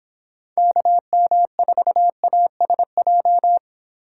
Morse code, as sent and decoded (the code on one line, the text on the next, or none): -.- -- ....- .- ... .---
KM4ASJ